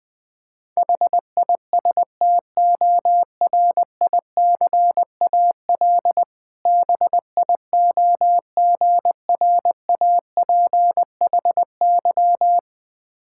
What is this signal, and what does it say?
Morse code, heard as a keyed tone, about 21 wpm, 700 Hz. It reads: HISTORICAL BIOGRAPHY
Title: Difficult Words with Hints - 20wpm